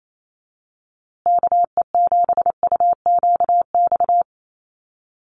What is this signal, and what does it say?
Morse code, heard as a keyed tone, about 28 wpm, 700 Hz. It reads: KE7UQX
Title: USA Call Signs Repeated 2x 28wpm